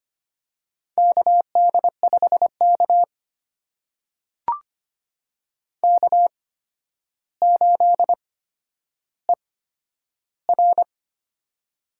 Morse code, heard as a keyed tone, about 25 wpm, spaced as for 11 wpm, 700 Hz. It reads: KD5K K8ER
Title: USA Call Signs - 4 Characters - 8x Character Spacing - 25wpm